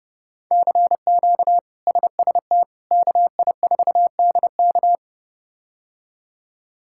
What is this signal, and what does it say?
Morse code, heard as a keyed tone, about 30 wpm, 700 Hz. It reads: CQ SST KI4DK
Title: K1USN SST QSO Practice - 30wpm